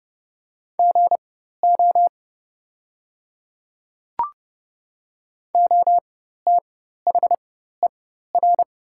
Morse code, heard as a keyed tone, about 31 wpm, spaced as for 14 wpm, 700 Hz. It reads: GO OTHER
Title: 4x Character Spacing - Top 100 Words - 30wpm